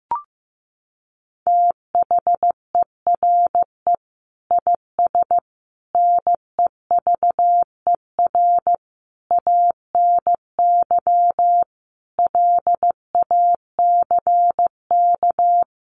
Morse code, heard as a keyed tone, about 15 wpm, 700 Hz. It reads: THERE IS NEVER ANY LACK
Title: Sentences from Top 1100 Words 15wpm